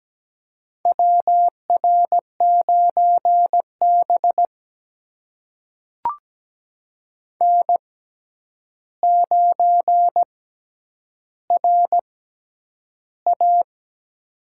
Morse code, heard as a keyed tone, about 17 wpm, spaced as for 9 wpm, 700 Hz. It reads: WR9B N9RA
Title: USA Call Signs - 4 Characters - 6x Character Spacing - 17wpm